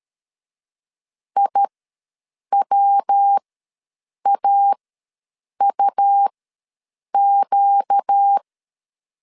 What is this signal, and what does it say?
Morse code, heard as a keyed tone, about 13 wpm, spaced as for 7 wpm, 800 Hz. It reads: IWAUQ
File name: LEZIONI / 7 GF9Z / 7 GF9 ws 12wpm 3344